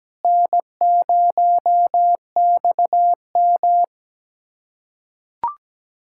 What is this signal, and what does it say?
Morse code, heard as a keyed tone, about 17 wpm, 700 Hz. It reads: N0XM
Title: USA Callsign DE Callsign - 4 Characters - 17wpm